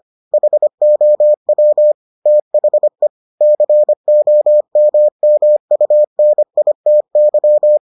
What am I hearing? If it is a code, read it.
Morse code, heard as a keyed tone, about 25 wpm, 600 Hz. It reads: HOW THE COMMUNITY